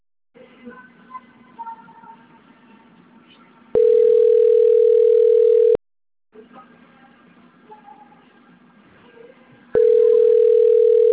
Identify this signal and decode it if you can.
Ringback tone (North America), 440 + 480 Hz.